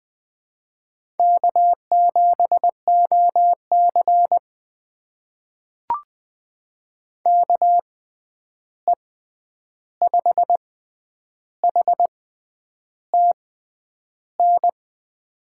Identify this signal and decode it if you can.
Morse code, heard as a keyed tone, about 20 wpm, spaced as for 9 wpm, 700 Hz. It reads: K7OC KE5HTN
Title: USA Call Signs - 6x Character Spacing - 20wpm